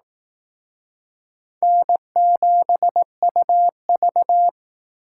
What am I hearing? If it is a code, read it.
Morse code, heard as a keyed tone, about 18 wpm, 700 Hz. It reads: N7UV
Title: Speed Racing - USA 1x2 Call Signs 15wpm